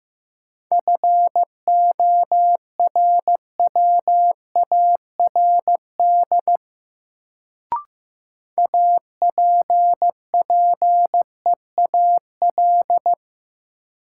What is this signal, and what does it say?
Morse code, heard as a keyed tone, about 15 wpm, 700 Hz. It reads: FORWARD APPEAL
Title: Sentences from Top 1300 Words - New Word Drill Alt - 15wpm